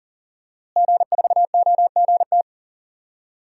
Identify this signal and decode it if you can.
Morse code, heard as a keyed tone, about 40 wpm, 700 Hz. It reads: G4OGT
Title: DXCC-100 Call Signs - 5 Characters - Repeated 2x - 40wpm